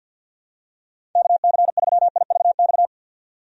Morse code, heard as a keyed tone, about 53 wpm, 700 Hz. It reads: KC3IVX
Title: USA Call Signs - 6 Characters - 50wpm